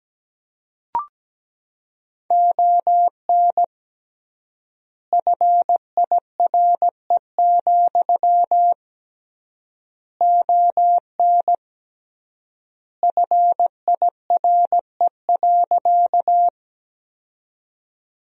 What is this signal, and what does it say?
Morse code, heard as a keyed tone, about 17 wpm, 700 Hz. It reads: ON FIRE, ON FIRE.
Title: 3x Word-Spacing - Sentences from Top 400 Words - 17wpm